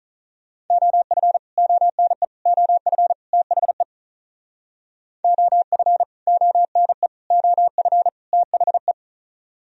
Morse code, heard as a keyed tone, about 46 wpm, 700 Hz. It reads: OF ONE OF THE OF ONE OF THE